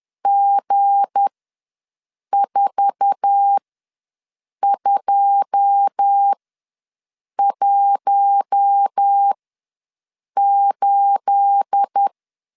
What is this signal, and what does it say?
Morse code, heard as a keyed tone, about 10 wpm, spaced as for 6 wpm, 800 Hz. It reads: G4218
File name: LEZIONI / 9 286 ws 10wpm 4032